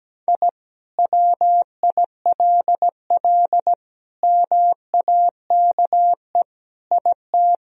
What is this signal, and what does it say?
Morse code, heard as a keyed tone, about 17 wpm, 700 Hz. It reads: I WILL MAKE IT
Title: Sentences from Top 1000 Words - Review Part 3 - 17wpm